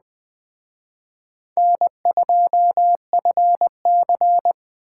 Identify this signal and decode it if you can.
Morse code, heard as a keyed tone, about 23 wpm, 700 Hz. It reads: N2FC
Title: Speed Racing - USA 1x2 Call Signs 20wpm